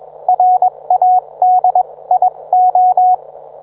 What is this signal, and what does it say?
Morse code, heard as a keyed tone, about 21 wpm, spaced as for 18 wpm, 730 Hz. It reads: RADIO
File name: book